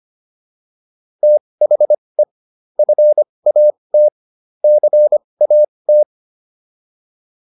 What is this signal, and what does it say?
Morse code, heard as a keyed tone, about 25 wpm, spaced as for 20 wpm, 600 Hz. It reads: THE FAT CAT